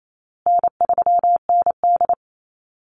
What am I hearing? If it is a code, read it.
Morse code, heard as a keyed tone, about 28 wpm, 700 Hz. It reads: N3ND
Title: Easy Call Signs Repeated 3x - 28wpm